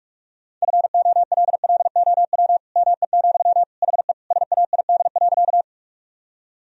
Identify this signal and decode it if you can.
Morse code, heard as a keyed tone, about 46 wpm, 700 Hz. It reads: FOLLOW ME, HE SAID.
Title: Sentences from Top 1000 Words - Review Part 1 - 45wpm